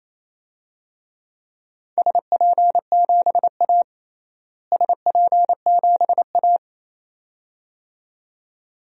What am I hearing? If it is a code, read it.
Morse code, heard as a keyed tone, about 28 wpm, 700 Hz. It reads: SP7A SP7A
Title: DXCC-100 Call Signs - 4 Characters - Repeated 2x - 28wpm